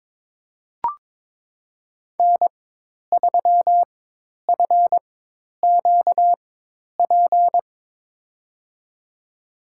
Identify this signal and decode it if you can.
Morse code, heard as a keyed tone, about 21 wpm, spaced as for 10 wpm, 700 Hz. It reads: N3FQP
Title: USA Call Signs - 5 Characters - 4x Character Spacing - 22wpm